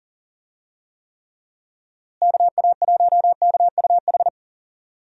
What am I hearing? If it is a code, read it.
Morse code, heard as a keyed tone, about 40 wpm, 700 Hz. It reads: KA1KUH
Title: USA Call Signs - 6 Characters - Repeated 3x - 40wpm